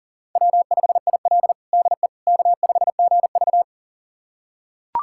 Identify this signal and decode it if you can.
Morse code, heard as a keyed tone, about 40 wpm, 700 Hz. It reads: W5IL DE K5GV
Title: USA Callsign DE Callsign - 4 Characters - 40wpm